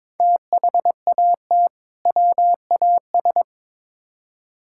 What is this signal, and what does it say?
Morse code, heard as a keyed tone, about 22 wpm, 700 Hz. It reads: THAT WAS
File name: practice / 22wpm / Build A Phrase - 2-Word Phrases - 22wpm